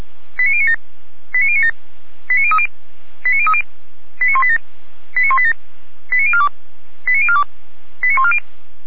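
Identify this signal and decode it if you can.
Five-tone selective call (ZVEI), heard as quick runs of five tones, tones of about 70 ms. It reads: 89097 89097 89030 89030 89187 89187 89042 89042 89140
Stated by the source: ZVEI five-tone call received by SDR.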